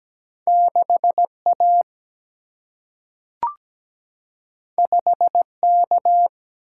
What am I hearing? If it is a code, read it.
Morse code, heard as a keyed tone, about 17 wpm, 700 Hz. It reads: 6A 5K
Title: Single-Letter Single-Number 17wpm